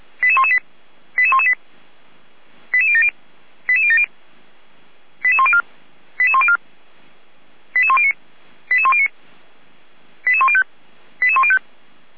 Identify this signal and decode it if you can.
Five-tone selective call (ZVEI), heard as quick runs of five tones, tones of about 80 ms. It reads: 88118 88118 88070 88070 88104 88104 88109 88109 88105 88105
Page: ZVEI